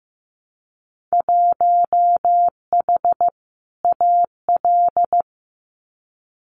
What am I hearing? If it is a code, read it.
Morse code, heard as a keyed tone, about 15 wpm, 700 Hz. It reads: 1H AL
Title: Winter Field Day Exchanges 15wpm